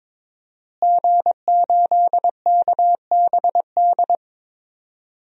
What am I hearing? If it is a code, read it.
Morse code, heard as a keyed tone, about 22 wpm, 700 Hz. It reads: G8KBD
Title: DXCC-100 Call Signs - 5 Characters - Repeated 2x - 22wpm